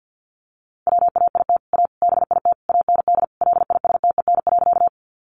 Morse code, heard as a keyed tone, about 50 wpm, 700 Hz. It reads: WAIT A BIT AND LISTEN.
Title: Sentences from Top 900 Words 50wpm